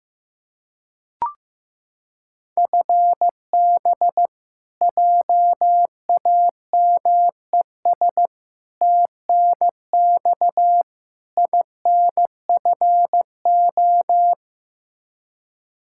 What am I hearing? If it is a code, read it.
Morse code, heard as a keyed tone, about 15 wpm, 700 Hz. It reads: FB JAMES TNX INFO
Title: Practice QSO - Exchange 3 - More Info - 15wpm